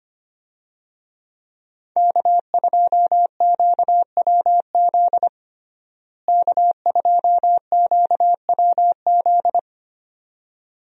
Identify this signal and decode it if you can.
Morse code, heard as a keyed tone, about 25 wpm, 700 Hz. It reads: K2QWZ K2QWZ
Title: USA Call Signs - 5 Characters - Repeated 2x - 25wpm